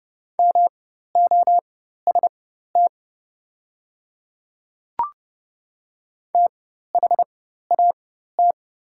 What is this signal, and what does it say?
Morse code, heard as a keyed tone, about 31 wpm, spaced as for 14 wpm, 690 Hz. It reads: MOST THAT
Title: 4x Character Spacing - Top 100 Words - 30wpm